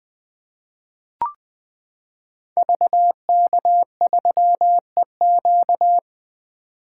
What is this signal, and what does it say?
Morse code, heard as a keyed tone, about 20 wpm, 700 Hz. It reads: VK3EQ
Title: 2023 SOTA Activator Call Signs - 20wpm